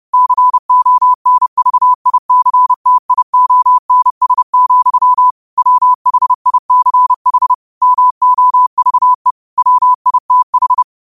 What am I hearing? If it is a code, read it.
Morse code, heard as a keyed tone, about 30 wpm, 1000 Hz. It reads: CONVICTIONS, WHICH MOVE WITH